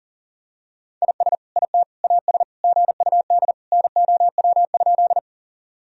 Morse code, heard as a keyed tone, about 37 wpm, 700 Hz. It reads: IS IT AS GUD NOW?
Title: Sentences from Top 200 Words and Abbreviations 40wpm